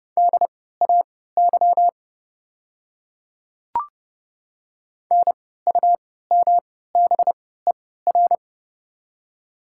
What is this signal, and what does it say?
Morse code, heard as a keyed tone, about 30 wpm, spaced as for 17 wpm, 700 Hz. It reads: DAY NUMBER
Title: 3x Character Spacing - Top 100 Words - 30wpm